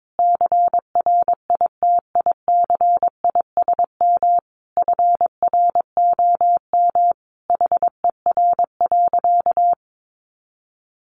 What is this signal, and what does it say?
Morse code, heard as a keyed tone, about 22 wpm, 700 Hz. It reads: CRITICISM FROM HER.
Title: Sentences from Top 1300 Words 22wpm